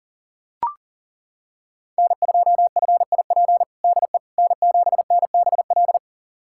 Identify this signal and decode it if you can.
Morse code, heard as a keyed tone, about 40 wpm, 700 Hz. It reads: N2FIP DE N7NBL
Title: USA Callsign DE Callsign - 5 Characters - 40wpm